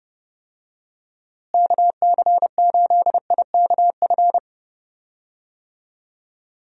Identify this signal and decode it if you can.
Morse code, heard as a keyed tone, about 30 wpm, 700 Hz. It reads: KC8IKF
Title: USA Call Signs - 6 Characters - Repeated 3x - 30wpm